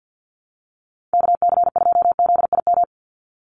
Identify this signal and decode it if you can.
Morse code, heard as a keyed tone, about 50 wpm, 700 Hz. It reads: KC2ZIM
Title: USA Call Signs 50wpm